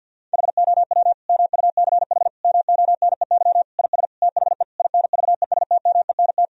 Morse code, heard as a keyed tone, about 50 wpm, 700 Hz. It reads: HOW MUCH MONEY IS THE INVESTMENT
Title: Sentences from Top 1000 Words - Review Part 4 - 50wpm